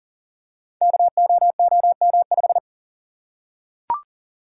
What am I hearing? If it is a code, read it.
Morse code, heard as a keyed tone, about 42 wpm, 690 Hz. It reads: KOOM5
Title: Code Groups - 5 Letter:Numbers - 40wpm